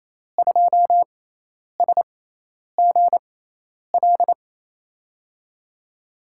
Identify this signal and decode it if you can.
Morse code, heard as a keyed tone, about 28 wpm, spaced as for 10 wpm, 700 Hz. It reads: 2SGL
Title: USA Call Signs - 6 Characters - 6x Character Spacing - 28wpm